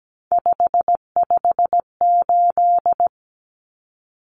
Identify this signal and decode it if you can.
Morse code, heard as a keyed tone, about 17 wpm, 700 Hz. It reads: 558
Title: Common Cut Numbers - 3-Digit 17wpm